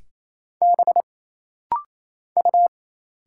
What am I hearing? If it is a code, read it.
Morse code, heard as a keyed tone, about 27 wpm, 700 Hz. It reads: B U
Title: Single Letters - Rapid-Fire 28wpm